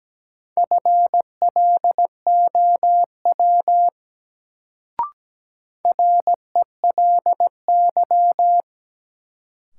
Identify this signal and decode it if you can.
Morse code, heard as a keyed tone, about 17 wpm, 700 Hz. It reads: FLOW RELY
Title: Sentences from Top 1200 Words - New Word Drill Alt - 17wpm